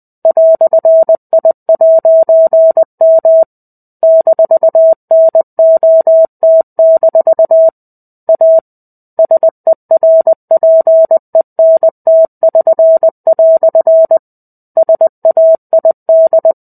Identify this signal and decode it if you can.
Morse code, heard as a keyed tone, about 20 wpm, 650 Hz. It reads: "I'M -NOT- A SERPENT<SN>" SAID
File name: 05-Chapter-05 20 WPM